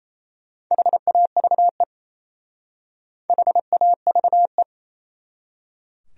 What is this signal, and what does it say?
Morse code, heard as a keyed tone, about 38 wpm, 700 Hz. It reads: HAVE HAVE